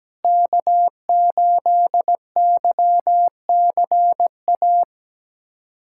Morse code, heard as a keyed tone, about 17 wpm, 700 Hz. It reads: K8YCA
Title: USA Callsign DE Callsign - 5 Characters - 17wpm